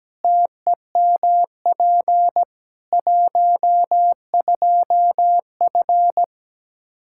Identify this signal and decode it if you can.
Morse code, heard as a keyed tone, about 17 wpm, 700 Hz. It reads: TEMP 12F
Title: K1USN SST QSO Practice 17wpm